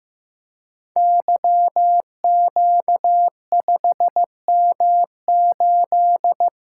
Morse code, heard as a keyed tone, about 15 wpm, 700 Hz. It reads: YQ5M8
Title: Code Groups - 5 Letter:Numbers - 15wpm